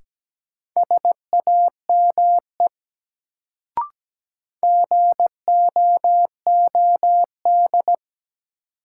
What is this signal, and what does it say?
Morse code, heard as a keyed tone, about 17 wpm, 700 Hz. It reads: SAME GOOD
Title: Sets of 2 Words - New Word Drill - 17wpm